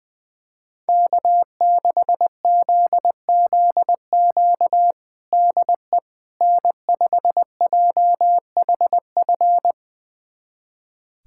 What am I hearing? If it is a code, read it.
Morse code, heard as a keyed tone, about 20 wpm, 700 Hz. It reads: K6ZZQ DE N5JHF
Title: USA Callsign DE Callsign - 5 Characters - 20wpm